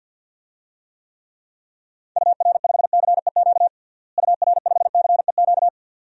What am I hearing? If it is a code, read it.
Morse code, heard as a keyed tone, about 53 wpm, 700 Hz. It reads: UR5CEQ UR5CEQ
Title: DXCC-100 Call Signs - 6 Characters - Repeated 2x - 50wpm